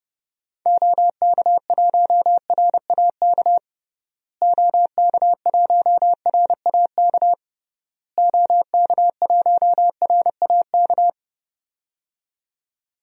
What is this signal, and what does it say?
Morse code, heard as a keyed tone, about 30 wpm, 700 Hz. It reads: OK1RAK OK1RAK OK1RAK
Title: DXCC-100 Call Signs - 6 Characters - Repeated 3x - 30wpm